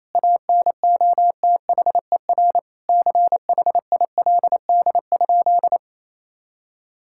Morse code, heard as a keyed tone, about 27 wpm, 700 Hz. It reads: ANOTHER CHILD?
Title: Sentences from Top 1600 Words 28wpm